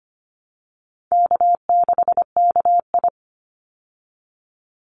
Morse code, heard as a keyed tone, about 25 wpm, 700 Hz. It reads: K6KI
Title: Easy Call Signs Repeated 2x - 25wpm